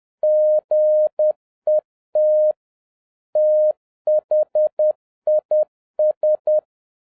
Morse code, heard as a keyed tone, about 10 wpm, 610 Hz. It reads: GET THIS
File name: 2011-07-mp3 / n56 BACK FROM A BIG ACCIDENT 10wpm